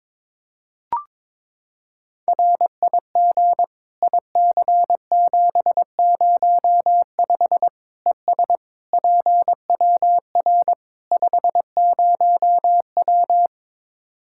Morse code, heard as a keyed tone, about 22 wpm, 700 Hz. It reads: RIG IC705 ES PWR 50W
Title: Practice QSO - Exchange 2 - Info - 22wpm